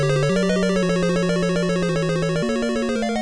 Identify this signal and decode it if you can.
There is no code.